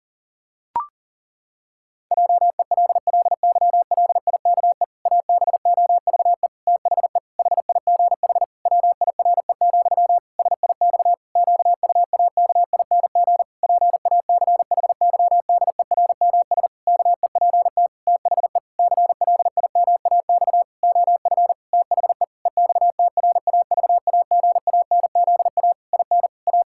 Morse code, heard as a keyed tone, about 40 wpm, 700 Hz. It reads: JELLYLIKE ABOVE THE HIGH WIRE, SIX QUAKING PACHYDERMS KEPT THE CLIMAX OF THE EXTRAVAGANZA IN A
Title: Pangram Challenge v2 - 40wpm